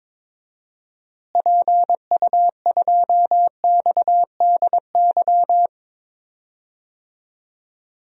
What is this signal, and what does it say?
Morse code, heard as a keyed tone, about 22 wpm, 700 Hz. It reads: PU2XDY
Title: DXCC-100 Call Signs - 6 Characters - Repeated 2x - 22wpm